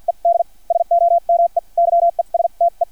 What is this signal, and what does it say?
Morse code, heard as a keyed tone, about 50 wpm, spaced as for 40 wpm, 680 Hz. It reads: ED SOME YESTE
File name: archived / CRSnet